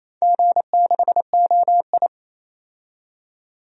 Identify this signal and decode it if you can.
Morse code, heard as a keyed tone, about 28 wpm, 700 Hz. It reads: G6OI
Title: DXCC-100 Call Signs - 4 Characters - 28wpm